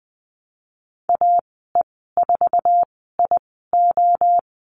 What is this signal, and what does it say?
Morse code, heard as a keyed tone, about 20 wpm, spaced as for 14 wpm, 700 Hz. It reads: AE4IO
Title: 055 Lesson 10g - Letter r - Call Signs 20wpm